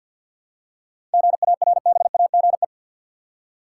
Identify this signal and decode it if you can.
Morse code, heard as a keyed tone, about 52 wpm, 700 Hz. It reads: GARBAGE
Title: Seven Letter Words - 50wpm